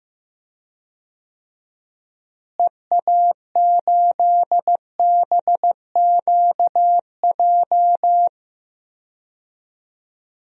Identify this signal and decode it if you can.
Morse code, heard as a keyed tone, about 15 wpm, 700 Hz. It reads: EA8BQJ
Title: DXCC-100 Call Signs - 6 Characters - Repeated 2x - 15wpm